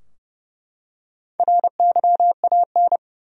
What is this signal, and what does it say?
Morse code, heard as a keyed tone, about 31 wpm, 700 Hz. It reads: RYAN
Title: USA Common Names - Max 4 Letters - 30wpm